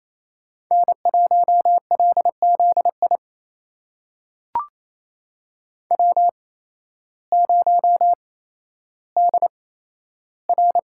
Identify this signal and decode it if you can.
Morse code, heard as a keyed tone, about 28 wpm, spaced as for 13 wpm, 700 Hz. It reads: N1LZI W0DR
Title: USA Call Signs - 8x Character Spacing - 28wpm